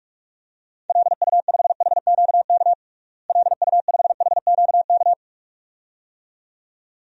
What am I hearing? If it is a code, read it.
Morse code, heard as a keyed tone, about 41 wpm, 700 Hz. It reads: LU5HQK LU5HQK
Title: DXCC-100 Call Signs - 6 Characters - Repeated 2x - 45wpm